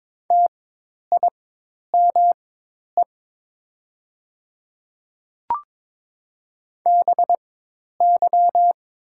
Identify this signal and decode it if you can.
Morse code, heard as a keyed tone, about 22 wpm, spaced as for 10 wpm, 700 Hz. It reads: TIME BY
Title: 4x Character Spacing - Top 100 Words - 22wpm